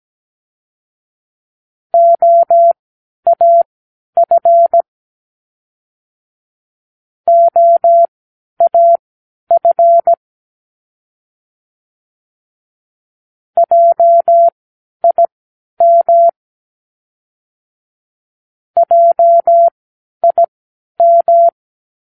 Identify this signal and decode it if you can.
Morse code, heard as a keyed tone, about 17 wpm, spaced as for 10 wpm, 680 Hz. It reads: OAF OAF JIM JIM
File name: k7qo / Practicewords